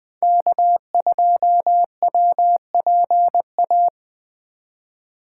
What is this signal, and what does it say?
Morse code, heard as a keyed tone, about 20 wpm, 700 Hz. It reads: K2WPA
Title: 2023 POTA Activator Call Signs - 20wpm